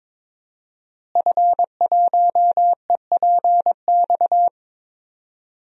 Morse code, heard as a keyed tone, about 23 wpm, 700 Hz. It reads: F1EPX
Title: DXCC-100 Call Signs - 5 Characters - Repeated 2x - 22wpm